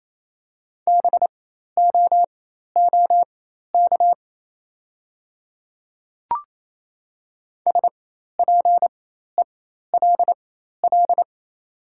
Morse code, heard as a keyed tone, about 28 wpm, spaced as for 13 wpm, 700 Hz. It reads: BOOK SPELL
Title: 4x Character Spacing - Top 300 Words - 28wpm